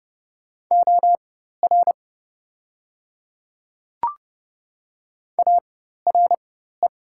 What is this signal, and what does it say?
Morse code, heard as a keyed tone, about 30 wpm, spaced as for 14 wpm, 700 Hz. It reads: OR ARE
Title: 4x Character Spacing - Top 100 Words - 30wpm